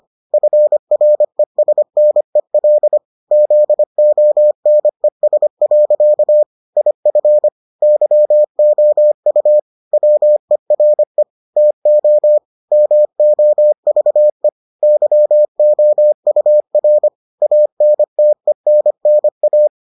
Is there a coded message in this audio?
Yes: Morse code from 0.3 to 19.7 s, heard as a keyed tone, about 25 wpm, 600 Hz. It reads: FRESNEL ZONES. IF YOU WERE TO MOVE YOUR ANTENNA